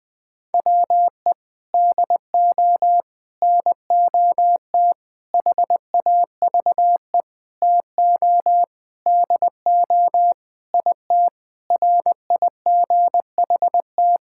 Morse code, heard as a keyed tone, about 21 wpm, 700 Hz. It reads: WE DO NOT HAVE TO DO IT RIGHT
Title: Sentences with 5-Word Phrases - 20wpm